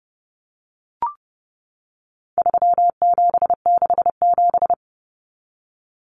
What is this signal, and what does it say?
Morse code, heard as a keyed tone, about 30 wpm, 700 Hz. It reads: 3767
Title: Numbers - 4-Digit 30wpm